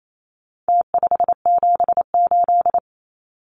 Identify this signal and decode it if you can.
Morse code, heard as a keyed tone, about 28 wpm, 700 Hz. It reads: T578
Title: Numbers - Contest Style - 4-Digit 28wpm